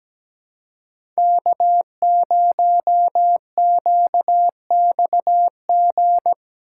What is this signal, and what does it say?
Morse code, heard as a keyed tone, about 17 wpm, 700 Hz. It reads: K0QXG
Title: USA Call Signs - 5 Characters - 2x Character Spacing - 17wpm